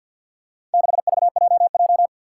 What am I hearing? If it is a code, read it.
Morse code, heard as a keyed tone, about 47 wpm, 700 Hz. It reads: 6VJJ
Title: Four Challenging Letter-Number Combinations 50wpm